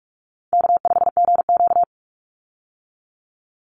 Morse code, heard as a keyed tone, about 44 wpm, 700 Hz. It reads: K5GQ
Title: Call Signs - 45wpm